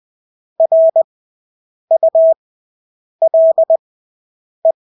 Morse code, heard as a keyed tone, about 20 wpm, spaced as for 8 wpm, 650 Hz. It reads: RULE
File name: Engelske ord 20/8 wpm